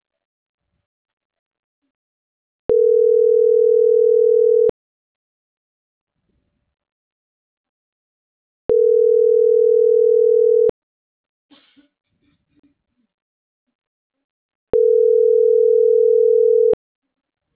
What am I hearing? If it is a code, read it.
Ringback tone (North America), 440 + 480 Hz.
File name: CENTRAL MAYAB / out dulce.silva - .24583